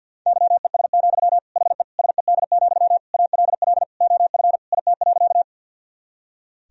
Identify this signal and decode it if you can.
Morse code, heard as a keyed tone, about 49 wpm, 700 Hz. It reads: YES, HE SED, ALL OF IT.
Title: Sentences from Top 500 Words and Abbreviations 50wpm